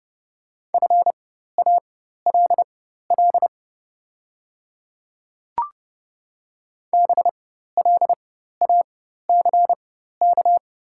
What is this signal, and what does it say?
Morse code, heard as a keyed tone, about 31 wpm, spaced as for 14 wpm, 690 Hz. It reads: FALL BLACK
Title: 4x Character Spacing - Top 500 Words - 30wpm